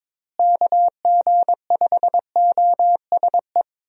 Morse code, heard as a keyed tone, about 22 wpm, 700 Hz. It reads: KG5OSE
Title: USA Call Signs - 6 Characters - Repeated 2x - 22wpm